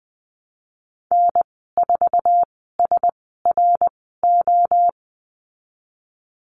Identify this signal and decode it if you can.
Morse code, heard as a keyed tone, about 20 wpm, 700 Hz. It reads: N 4 S R O
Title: 062 Lesson 11g - Letter h - Call Signs 20wpm